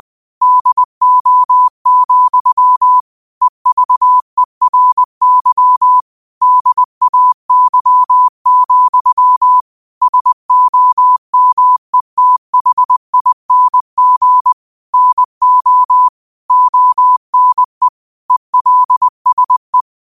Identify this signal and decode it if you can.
Morse code, heard as a keyed tone, about 20 wpm, 1000 Hz. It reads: DO, EVERY DAY, SOMETHING NO ONE ELSE
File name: qotdfiles / 20wpm